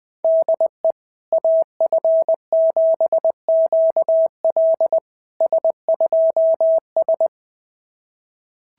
Morse code, heard as a keyed tone, about 20 wpm, 660 Hz. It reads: DE AF7QL S2S
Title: SOTA QSO Practice 20wpm